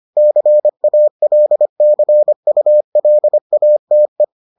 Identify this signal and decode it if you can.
Morse code, heard as a keyed tone, about 25 wpm, 600 Hz. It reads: CALCULATE